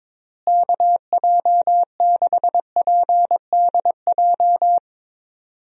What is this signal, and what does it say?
Morse code, heard as a keyed tone, about 22 wpm, 700 Hz. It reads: KJ6PDJ
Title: USA Call Signs - 6 Characters - Repeated 2x - 22wpm